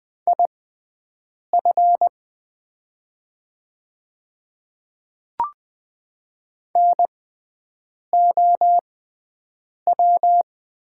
Morse code, heard as a keyed tone, about 20 wpm, spaced as for 7 wpm, 690 Hz. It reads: IF NOW
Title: 6x Character Spacing - Top 100 Words - 20wpm